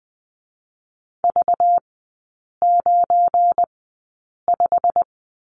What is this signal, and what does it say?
Morse code, heard as a keyed tone, about 20 wpm, spaced as for 8 wpm, 700 Hz. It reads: V95
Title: 198 Lesson 32e - Letter k - Three Characters 20wpm